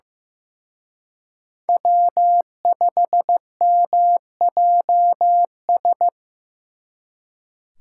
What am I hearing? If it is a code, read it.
Morse code, heard as a keyed tone, about 18 wpm, 700 Hz. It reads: W5MJS
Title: Speed Racing - USA 1x3 Call Signs 15wpm